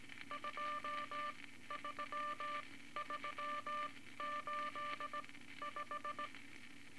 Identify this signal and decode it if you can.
Morse code, heard as a keyed tone, about 18 wpm, spaced as for 13 wpm, 1180 Hz, with noise. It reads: 23385
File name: Morse Code Records / 18 A LCM 164 Ten Words Per Minute